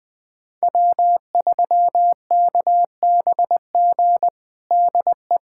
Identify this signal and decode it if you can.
Morse code, heard as a keyed tone, about 20 wpm, 700 Hz. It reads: W3KBG DE
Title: USA Callsign DE Callsign - 5 Characters - 20wpm